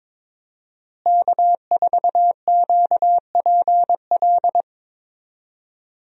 Morse code, heard as a keyed tone, about 22 wpm, 700 Hz. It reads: K4QPL
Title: USA Call Signs - 5 Characters - Repeated 2x - 22wpm